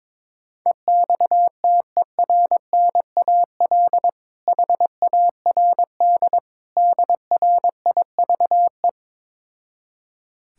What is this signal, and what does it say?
Morse code, heard as a keyed tone, about 22 wpm, 700 Hz. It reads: EXTERNAL HARD DRIVE
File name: Difficult Words with Hints - 22wpm